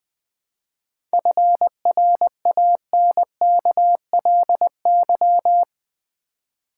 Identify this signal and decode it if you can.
Morse code, heard as a keyed tone, about 20 wpm, 700 Hz. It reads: FRANKLY
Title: 7 Letter Words - First 3 Letters Given - 20wpm